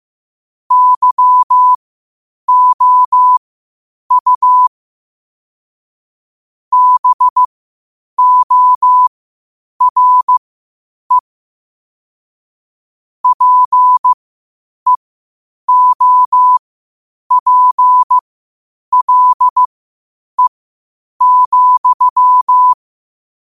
Morse code, heard as a keyed tone, about 15 wpm, spaced as for 8 wpm, 1000 Hz. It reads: YOU BORE PEOPLE,
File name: qotdfiles / 8wpm